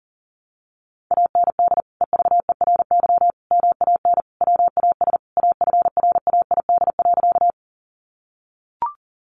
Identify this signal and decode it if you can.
Morse code, heard as a keyed tone, about 40 wpm, 700 Hz. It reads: AND EVERY MAN WAS AFRAID.
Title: Sentences from Top 1500 Words 40wpm